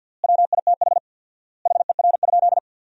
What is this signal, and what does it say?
Morse code, heard as a keyed tone, about 47 wpm, 700 Hz. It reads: WITH HER?
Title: 4x Word-Spacing - Sentences from Top 400 Words - 50wpm